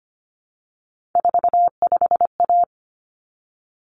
Morse code, heard as a keyed tone, about 25 wpm, 700 Hz. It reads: 45A
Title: Common Cut Numbers - 3-Digit 25wpm